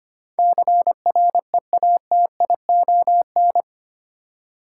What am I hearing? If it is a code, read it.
Morse code, heard as a keyed tone, about 25 wpm, 700 Hz. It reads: CREATION
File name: 8 Letter Words - First 3 Letters Given - 25wpm